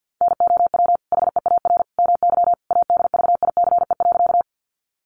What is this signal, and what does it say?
Morse code, heard as a keyed tone, about 50 wpm, 700 Hz. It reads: NOW HEAR MY ADVICE.
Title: Sentences from Top 1100 Words 50wpm